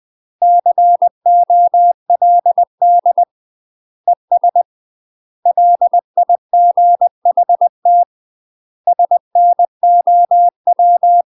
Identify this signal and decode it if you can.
Morse code, heard as a keyed tone, about 20 wpm, 700 Hz. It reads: COLD ES LIGHT SNOW
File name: UK MASTER QSOs TRAINING/20WPM